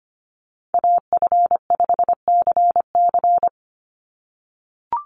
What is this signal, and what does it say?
Morse code, heard as a keyed tone, about 25 wpm, 700 Hz. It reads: AF5CC
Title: Call Signs - 25wpm